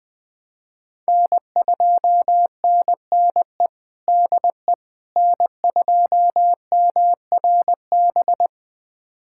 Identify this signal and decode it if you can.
Morse code, heard as a keyed tone, about 21 wpm, 700 Hz. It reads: N2NNE DE N2MRB
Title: USA Callsign DE Callsign - 5 Characters - 20wpm